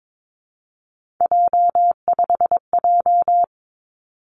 Morse code, heard as a keyed tone, about 22 wpm, 700 Hz. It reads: J5J
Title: Three Challenging Letter-Number Combinations 22wpm